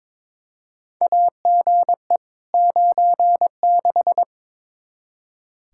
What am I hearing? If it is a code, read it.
Morse code, heard as a keyed tone, about 22 wpm, 700 Hz. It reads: AGE 96
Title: Practice QSO - Exchange 3 - More Info - 22wpm